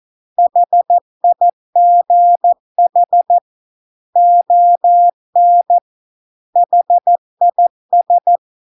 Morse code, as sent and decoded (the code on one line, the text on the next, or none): .... .. --. .... / --- -. / .... .. ...
HIGH ON HIS